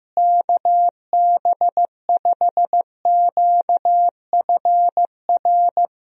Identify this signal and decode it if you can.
Morse code, heard as a keyed tone, about 15 wpm, 700 Hz. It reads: KB5QFR
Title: USA Call Signs - 6 Characters - Repeated 3x - 15wpm